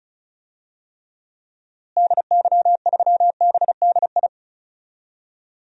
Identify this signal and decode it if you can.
Morse code, heard as a keyed tone, about 36 wpm, 700 Hz. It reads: DY3BDI
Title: DXCC-100 Call Signs - 6 Characters - Repeated 3x - 35wpm